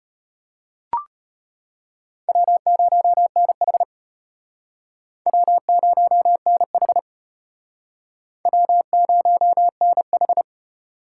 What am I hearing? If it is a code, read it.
Morse code, heard as a keyed tone, about 32 wpm, 700 Hz. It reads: W0NH W0NH W0NH